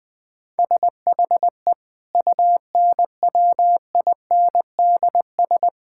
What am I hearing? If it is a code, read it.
Morse code, heard as a keyed tone, about 20 wpm, 700 Hz. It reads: SHE UNWINDS
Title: Difficult Sentences with Hints - 20wpm